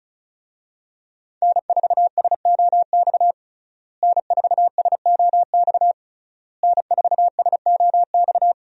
Morse code, heard as a keyed tone, about 36 wpm, 700 Hz. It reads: N4SOX N4SOX N4SOX
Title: USA Call Signs - 5 Characters - Repeated 3x - 35wpm